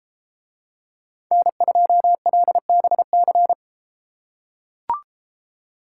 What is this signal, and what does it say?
Morse code, heard as a keyed tone, about 32 wpm, 700 Hz. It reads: N2LBC